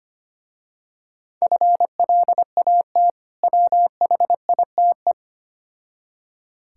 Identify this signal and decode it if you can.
Morse code, heard as a keyed tone, about 25 wpm, 700 Hz. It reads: FLAT WHITE
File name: practice / 25wpm / Difficult Words with Hints - 25wpm